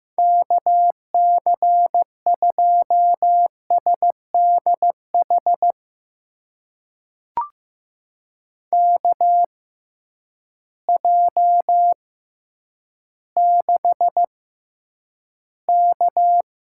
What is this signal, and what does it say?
Morse code, heard as a keyed tone, about 15 wpm, spaced as for 9 wpm, 700 Hz. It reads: KC2SDH KJ6K
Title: USA Call Signs - 6 Characters - 6x Character Spacing - 15wpm